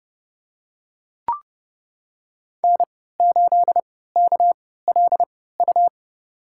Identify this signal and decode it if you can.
Morse code, heard as a keyed tone, about 30 wpm, spaced as for 17 wpm, 700 Hz. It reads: N8KLU
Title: USA Call Signs - 5 Characters - 3x Character Spacing - 30wpm